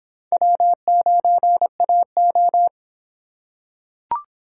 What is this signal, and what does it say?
Morse code, heard as a keyed tone, about 26 wpm, 700 Hz. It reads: W9AO